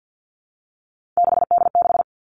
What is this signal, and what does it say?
Morse code, heard as a keyed tone, about 50 wpm, 700 Hz. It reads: 6D6
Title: Three Challenging Letter-Number Combinations 50wpm